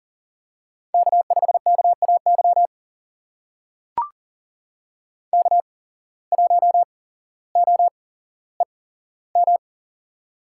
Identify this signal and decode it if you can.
Morse code, heard as a keyed tone, about 39 wpm, spaced as for 17 wpm, 700 Hz. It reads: K5KAY K1OEM
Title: USA Call Signs - 5 Characters - 8x Character Spacing - 40wpm